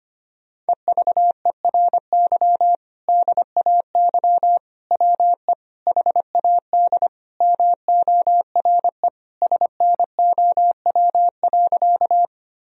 Morse code, heard as a keyed tone, about 25 wpm, 700 Hz. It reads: EVERY DAY WE HAD MORE SNOW.
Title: Sentences from Top 1000 Words - Review Part 3 - 25wpm